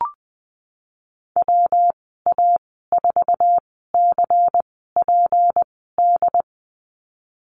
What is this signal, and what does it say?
Morse code, heard as a keyed tone, about 20 wpm, spaced as for 15 wpm, 700 Hz. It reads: WA4CPD
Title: 207 Lesson 33g - Letter j - Call Signs 20wpm